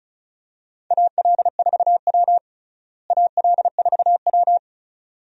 Morse code, heard as a keyed tone, about 34 wpm, 700 Hz. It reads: AL4W AL4W
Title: USA Call Signs - 4 Characters - Repeated 2x - 35wpm